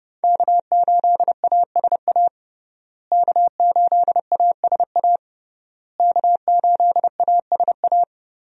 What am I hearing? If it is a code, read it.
Morse code, heard as a keyed tone, about 32 wpm, 700 Hz. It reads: K8ASA K8ASA K8ASA
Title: USA Call Signs - 5 Characters - Repeated 3x - 30wpm